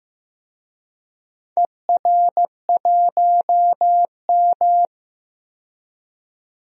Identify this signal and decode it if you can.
Morse code, heard as a keyed tone, about 15 wpm, 700 Hz. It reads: ER1M
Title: DXCC-100 Call Signs - 4 Characters - 15wpm